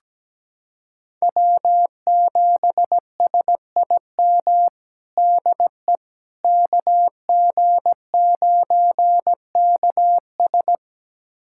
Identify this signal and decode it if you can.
Morse code, heard as a keyed tone, about 17 wpm, 700 Hz. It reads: W7SIM DE KG9KS
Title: USA Callsign DE Callsign - 5 Characters - 17wpm